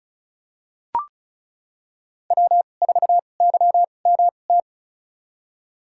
Morse code, heard as a keyed tone, about 34 wpm, spaced as for 25 wpm, 700 Hz. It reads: W4YMT
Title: USA Call Signs - 5 Characters - 2x Character Spacing - 35wpm